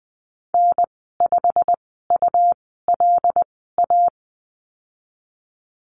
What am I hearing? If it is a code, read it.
Morse code, heard as a keyed tone, about 20 wpm, spaced as for 14 wpm, 700 Hz. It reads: N5ULA
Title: 167 Lesson 27g - Number 7 - Call Signs 20wpm